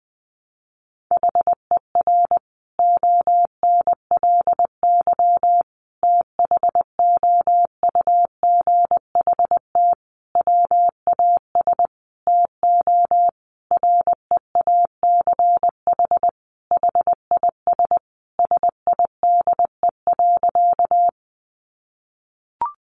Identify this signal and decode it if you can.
Morse code, heard as a keyed tone, about 20 wpm, 700 Hz. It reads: HER ONLY THOUGHT WAS TO REACH HIS SIDE.
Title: Sentences from Top 400 Words 20wpm